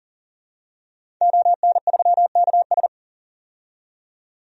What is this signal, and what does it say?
Morse code, heard as a keyed tone, about 40 wpm, 700 Hz. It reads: ON3KS
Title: DXCC-100 Call Signs - 5 Characters - 40wpm